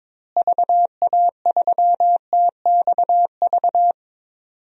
Morse code, heard as a keyed tone, about 22 wpm, 700 Hz. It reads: VA3TXV
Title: DXCC-100 Call Signs - 6 Characters - Repeated 2x - 22wpm